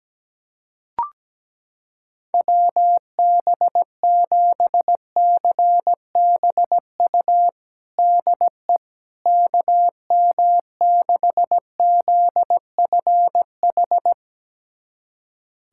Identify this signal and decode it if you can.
Morse code, heard as a keyed tone, about 17 wpm, 700 Hz. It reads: WB7CBU DE KM6ZFH
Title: USA Callsign DE Callsign - 6 Characters - 17wpm